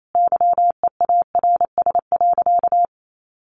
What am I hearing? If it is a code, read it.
Morse code, heard as a keyed tone, about 28 wpm, 700 Hz. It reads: YEARS.
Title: Sentences from Top 200 Words 28wpm